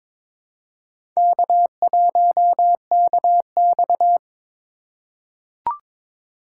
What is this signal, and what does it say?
Morse code, heard as a keyed tone, about 22 wpm, 700 Hz. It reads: K1KX
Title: USA Call Signs - 4 Characters - 4x Character Spacing - 22wpm